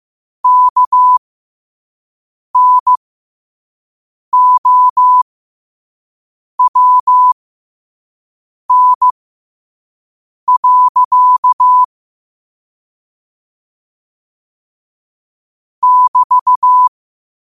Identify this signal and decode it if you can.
Morse code, heard as a keyed tone, about 15 wpm, spaced as for 5 wpm, 1000 Hz. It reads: KNOWN. <BT>
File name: qotdfiles / 5wpm